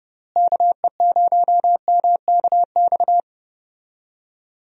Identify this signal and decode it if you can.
Morse code, heard as a keyed tone, about 30 wpm, 700 Hz. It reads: KE0MKX
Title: USA Call Signs - 6 Characters - Repeated 3x - 30wpm